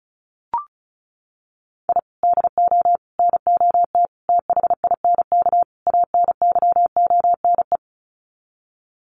Morse code, heard as a keyed tone, about 34 wpm, 700 Hz. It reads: I DO NOT THINK ANYONE
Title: Sets of 5 Words Encore 35wpm